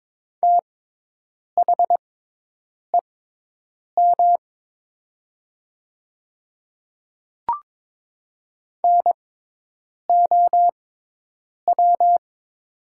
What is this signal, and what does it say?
Morse code, heard as a keyed tone, about 22 wpm, spaced as for 8 wpm, 700 Hz. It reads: THEM NOW
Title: 6x Character Spacing - Top 100 Words - 22wpm